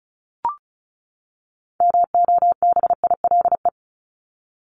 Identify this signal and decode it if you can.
Morse code, heard as a keyed tone, about 34 wpm, 700 Hz. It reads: MOBILE
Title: Top 1751-2000 Words 35wpm